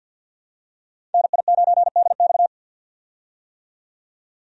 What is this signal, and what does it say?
Morse code, heard as a keyed tone, about 46 wpm, 700 Hz. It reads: NI9DX
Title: USA Call Signs - 5 Characters - 50wpm